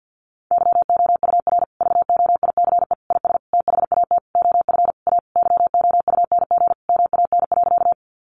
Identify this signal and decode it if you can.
Morse code, heard as a keyed tone, about 50 wpm, 700 Hz. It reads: YOUR VOICE IS THAT OF A YOUNG MAN.
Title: Sentences from Top 500 Words 50wpm